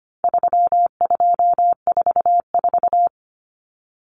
Morse code, heard as a keyed tone, about 25 wpm, 700 Hz. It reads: 3244
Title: Common Cut Numbers - 4-Digit 25wpm